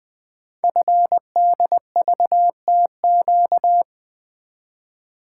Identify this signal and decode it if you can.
Morse code, heard as a keyed tone, about 20 wpm, 700 Hz. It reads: FDVTQ
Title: Code Groups - 5 Letters - 20wpm